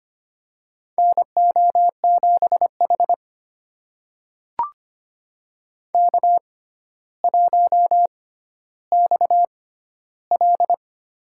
Morse code, heard as a keyed tone, about 25 wpm, spaced as for 13 wpm, 700 Hz. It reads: NO7H K1XL
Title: USA Call Signs - 4 Characters - 6x Character Spacing - 25wpm